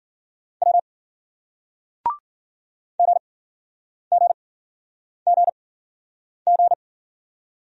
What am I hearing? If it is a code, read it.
Morse code, heard as a keyed tone, about 52 wpm, spaced as for 9 wpm, 700 Hz. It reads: W GGGG